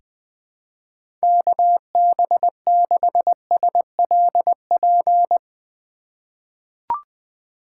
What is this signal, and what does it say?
Morse code, heard as a keyed tone, about 20 wpm, 700 Hz. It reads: KB6SLP
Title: USA Call Signs - 6 Characters - 3x Character Spacing - 20wpm